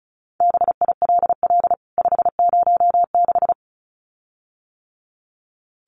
Morse code, heard as a keyed tone, about 34 wpm, 700 Hz. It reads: BILL 506
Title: CWT Exchange 35wpm